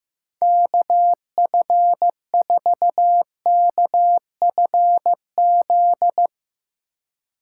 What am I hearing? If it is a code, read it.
Morse code, heard as a keyed tone, about 15 wpm, 700 Hz. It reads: KF4KFZ
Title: Speed Racing - USA 2x3 Call Signs 15wpm